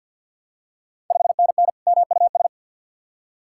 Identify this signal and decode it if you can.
Morse code, heard as a keyed tone, about 51 wpm, 690 Hz. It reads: 5NN MUS
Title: Ontario QSO Party Exchange 50wpm